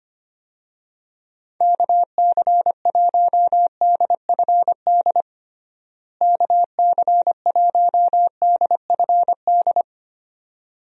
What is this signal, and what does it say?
Morse code, heard as a keyed tone, about 25 wpm, 700 Hz. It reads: KC1DFD KC1DFD
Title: USA Call Signs - 6 Characters - Repeated 2x - 25wpm